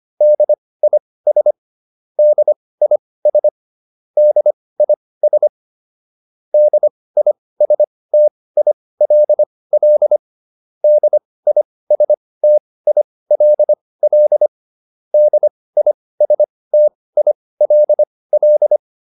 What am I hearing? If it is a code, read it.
Morse code, heard as a keyed tone, about 25 wpm, spaced as for 18 wpm, 600 Hz. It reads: DIS DIS DIS DISTILL DISTILL DISTILL